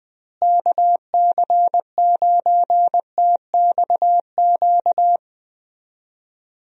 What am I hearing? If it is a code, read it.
Morse code, heard as a keyed tone, about 20 wpm, 700 Hz. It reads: KC9TXQ
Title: USA Call Signs - 6 Characters - Repeated 2x - 20wpm